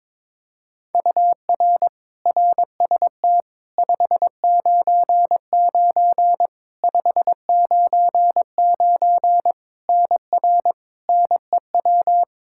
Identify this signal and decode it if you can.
Morse code, heard as a keyed tone, about 22 wpm, 700 Hz. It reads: UR RST 599 599 NR NEW
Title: Practice QSO - Exchange 1 - Beginning - 22wpm